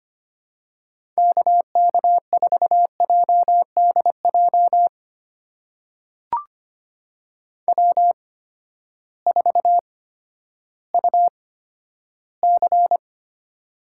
Morse code, heard as a keyed tone, about 25 wpm, spaced as for 12 wpm, 700 Hz. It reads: KK4JDJ W4UC
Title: USA Call Signs - 8x Character Spacing - 25wpm